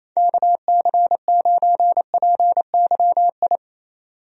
Morse code, heard as a keyed tone, about 27 wpm, 700 Hz. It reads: KC9PYI
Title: USA Call Signs - 6 Characters - 28wpm